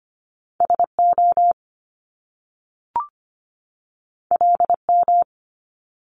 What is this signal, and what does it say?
Morse code, heard as a keyed tone, about 25 wpm, 700 Hz. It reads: SO LM
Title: Top Two Letter Combinations in Words 25wpm